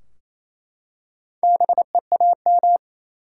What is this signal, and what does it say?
Morse code, heard as a keyed tone, about 28 wpm, 700 Hz. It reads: BEAM
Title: Top 90 QSO Elements 28wpm